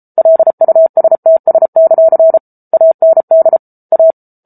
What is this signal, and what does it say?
Morse code, heard as a keyed tone, about 34 wpm, 650 Hz. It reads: LUSTS; AND A